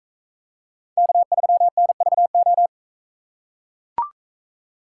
Morse code, heard as a keyed tone, about 44 wpm, 700 Hz. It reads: K3NVO